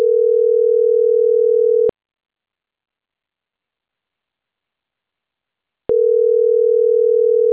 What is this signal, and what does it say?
Ringback tone (North America), 440 + 480 Hz.